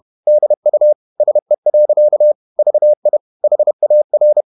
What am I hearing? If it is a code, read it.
Morse code, heard as a keyed tone, about 31 wpm, 600 Hz. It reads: DU SE. VI HAR
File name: Ferie QRQ 31wpm